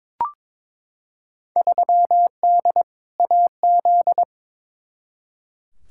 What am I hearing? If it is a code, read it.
Morse code, heard as a keyed tone, about 22 wpm, 700 Hz. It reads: 3D AZ
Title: Field Day Exchanges - 22wpm